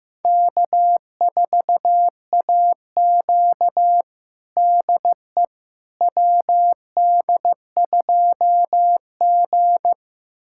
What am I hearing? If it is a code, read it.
Morse code, heard as a keyed tone, about 16 wpm, 700 Hz. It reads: K4AQ DE WD2G
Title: USA Callsign DE Callsign - 4 Characters - 15wpm